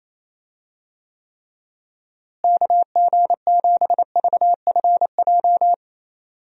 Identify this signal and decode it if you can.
Morse code, heard as a keyed tone, about 28 wpm, 700 Hz. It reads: KG7VFJ
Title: USA Call Signs - 6 Characters - Repeated 3x - 28wpm